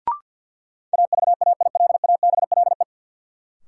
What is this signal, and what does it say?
Morse code, heard as a keyed tone, about 53 wpm, 700 Hz. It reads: AVAILABLE
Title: Sentences from Top 600 Words - New Word Drill - 50wpm